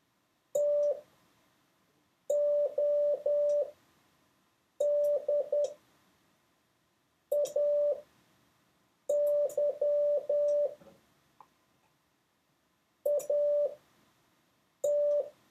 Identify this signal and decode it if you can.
Morse code, heard as a keyed tone, about 10 wpm, spaced as for 5 wpm, 580 Hz. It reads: TODAY AT
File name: 6of6-test